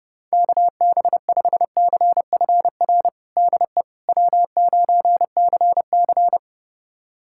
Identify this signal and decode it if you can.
Morse code, heard as a keyed tone, about 29 wpm, 700 Hz. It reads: KB5CFR DE W9CC
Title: USA Callsign DE Callsign 30wpm